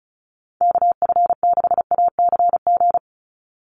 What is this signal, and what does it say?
Morse code, heard as a keyed tone, about 34 wpm, 700 Hz. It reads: KF6ACG
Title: USA Call Signs 35wpm